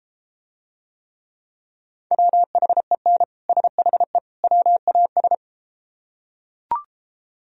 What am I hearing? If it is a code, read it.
Morse code, heard as a keyed tone, about 34 wpm, 700 Hz. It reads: WHEN SHE WAS